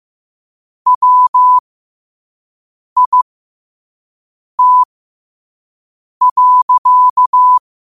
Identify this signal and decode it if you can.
Morse code, heard as a keyed tone, about 15 wpm, spaced as for 5 wpm, 1000 Hz. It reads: WIT.
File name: qotdfiles / 5wpm